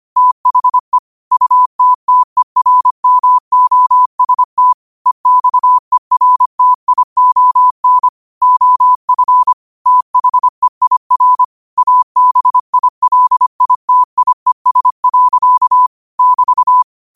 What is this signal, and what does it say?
Morse code, heard as a keyed tone, about 25 wpm, 1000 Hz. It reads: THE UTTERMOST EXERTION OF THEIR ABILITIES. <BT>
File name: qotdfiles / 25wpm